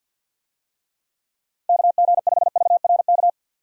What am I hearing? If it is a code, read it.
Morse code, heard as a keyed tone, about 51 wpm, 700 Hz. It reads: KG5VRK
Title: USA Call Signs - 6 Characters - Repeated 2x - 50wpm